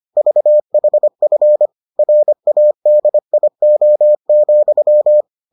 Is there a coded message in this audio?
Yes: Morse code from 0.2 to 5.2 s, heard as a keyed tone, about 25 wpm, 600 Hz. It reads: VHF RADIO,